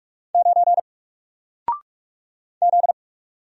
Morse code, heard as a keyed tone, about 45 wpm, 700 Hz. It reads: 9 Z
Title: Single Character (Letter:Number:Punctuation:Prosign) Alt 45wpm